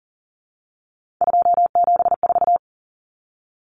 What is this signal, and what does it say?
Morse code, heard as a keyed tone, about 40 wpm, 700 Hz. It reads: 274
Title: Numbers - 3-Digit 40wpm